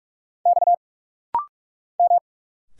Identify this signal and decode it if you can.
Morse code, heard as a keyed tone, about 43 wpm, 700 Hz. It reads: X M
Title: Single Letter-Number Alt - Rapid-Fire 45wpm